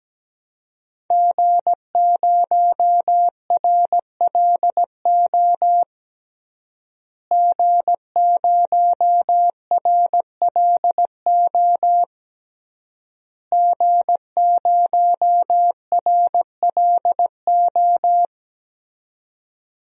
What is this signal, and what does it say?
Morse code, heard as a keyed tone, about 17 wpm, 700 Hz. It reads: G0RLO G0RLO G0RLO
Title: DXCC-100 Call Signs - 5 Characters - Repeated 3x - 17wpm